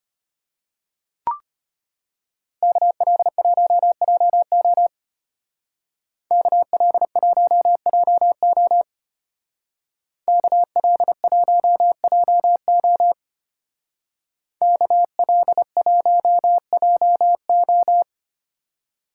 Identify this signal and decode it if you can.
Morse code, heard as a keyed tone, about 32 wpm, 700 Hz. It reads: KL1JO KL1JO KL1JO KL1JO